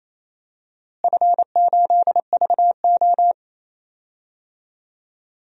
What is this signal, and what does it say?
Morse code, heard as a keyed tone, about 28 wpm, 700 Hz. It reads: F8VO
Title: DXCC-100 Call Signs - 4 Characters - Repeated 3x - 28wpm